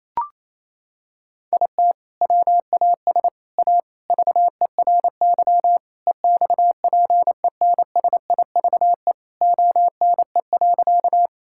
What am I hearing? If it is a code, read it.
Morse code, heard as a keyed tone, about 27 wpm, 700 Hz. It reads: IT WAS A VERY EXPENSIVE ONE.
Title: Sentences from Top 1000 Words - Review Part 6 - 28wpm